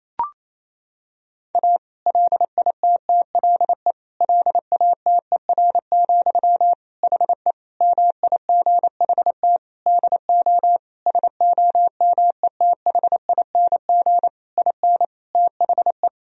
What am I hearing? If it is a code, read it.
Morse code, heard as a keyed tone, about 27 wpm, 700 Hz. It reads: A LITTLE LATER, HE MIGHT DO SOMETHING IN THE
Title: Sentences from Top 1000 Words - Review Part 4 - 28wpm